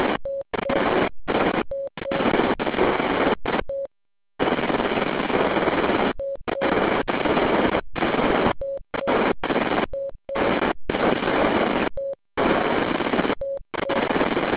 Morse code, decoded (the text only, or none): none